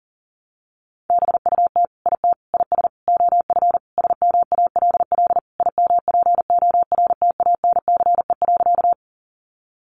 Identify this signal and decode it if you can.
Morse code, heard as a keyed tone, about 40 wpm, 700 Hz. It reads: BUT IT IS OF SMALL IMPORTANCE.
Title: Sentences from Top 1200 Words 40wpm